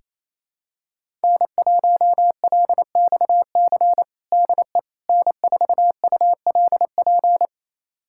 Morse code, heard as a keyed tone, about 27 wpm, 700 Hz. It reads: N1LXC DE N4ULP
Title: USA Callsign DE Callsign - 5 Characters - 28wpm